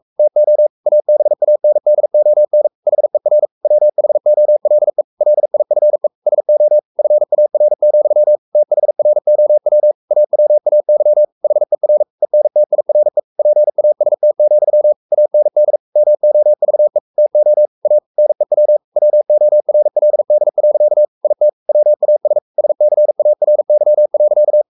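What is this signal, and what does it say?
Morse code, heard as a keyed tone, about 44 wpm, 600 Hz. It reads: TO ABANDON HER WHOLE LIFE SO FAR, THROW AWAY HER ENTIRE PAST, AND MOVE TO A NEW WORLD. IT WAS SCARY.
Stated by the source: Morse code at 43 wpm sending Ken Follett's novel Never in English, from CW med Gnister.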